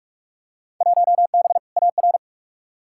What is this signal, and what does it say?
Morse code, heard as a keyed tone, about 45 wpm, 700 Hz. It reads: 1B AR
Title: Field Day Exchanges Encore - 45wpm